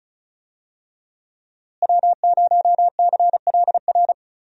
Morse code, heard as a keyed tone, about 34 wpm, 700 Hz. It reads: W0CLR
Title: USA Call Signs - 5 Characters - Repeated 3x - 35wpm